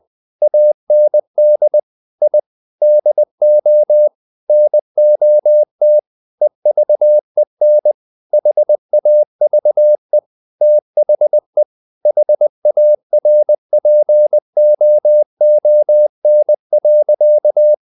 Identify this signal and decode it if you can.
Morse code, heard as a keyed tone, about 20 wpm, 600 Hz. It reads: AND I DO NOT EVEN HAVE THE HARPOON.
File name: the old man and the sea part 2 20wpm